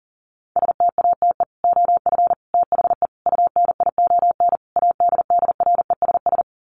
Morse code, heard as a keyed tone, about 40 wpm, 700 Hz. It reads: STATE OF THE UNION ADDRESS
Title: Sets of 5 Words Encore 40wpm